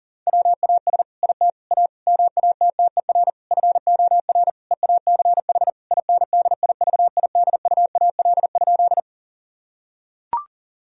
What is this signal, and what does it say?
Morse code, heard as a keyed tone, about 40 wpm, 700 Hz. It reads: WAS IT A MATTER FOR EACH INDIVIDUAL?
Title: Sentences from Top 1000 Words - Review Part 4 - 40wpm